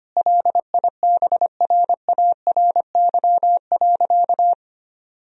Morse code, heard as a keyed tone, about 25 wpm, 700 Hz. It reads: LIBRARY.
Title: Sentences from Top 1000 Words - Review Part 4 - 25wpm